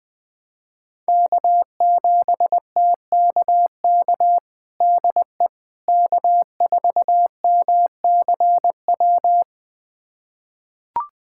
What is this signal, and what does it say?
Morse code, heard as a keyed tone, about 20 wpm, 700 Hz. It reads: K7TKK DE K4MCW
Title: USA Callsign DE Callsign - 5 Characters - 20wpm